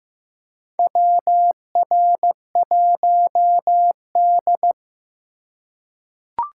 Morse code, heard as a keyed tone, about 15 wpm, 700 Hz. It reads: WR1D
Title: USA Call Signs - 4 Characters - 2x Character Spacing - 15wpm